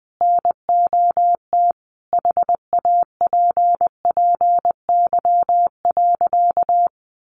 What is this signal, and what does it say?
Morse code, heard as a keyed tone, about 20 wpm, 700 Hz. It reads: NOT HAPPY.
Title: Sentences from Top 1400 Words 20wpm